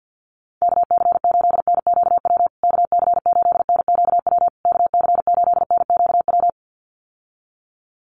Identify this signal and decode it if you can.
Morse code, heard as a keyed tone, about 51 wpm, 700 Hz. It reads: KC8NQW KC8NQW KC8NQW
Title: USA Call Signs Repeated 3x 50wpm